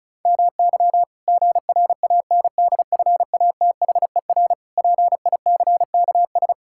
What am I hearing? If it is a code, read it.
Morse code, heard as a keyed tone, about 36 wpm, 700 Hz. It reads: MY GRANDFATHER PICKS
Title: Pangram Challenge v2 - 35wpm